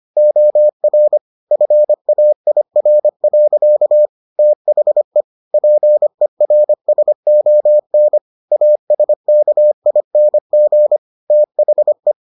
Morse code, heard as a keyed tone, about 25 wpm, 600 Hz. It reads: OR FAIR. THE PERSON ASKING THE